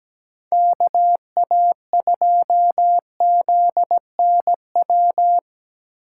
Morse code, heard as a keyed tone, about 17 wpm, 700 Hz. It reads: KA2ZNW
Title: USA Call Signs - 6 Characters - 2x Character Spacing - 17wpm